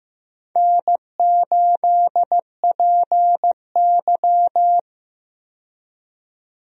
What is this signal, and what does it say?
Morse code, heard as a keyed tone, about 15 wpm, 700 Hz. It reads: N8PY
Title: USA Call Signs - 4 Characters - 15wpm